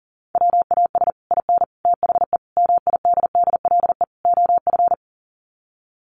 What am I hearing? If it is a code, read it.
Morse code, heard as a keyed tone, about 40 wpm, 700 Hz. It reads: WAS IN THE MIDDLE OF
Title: Sets of 5 Words Encore 40wpm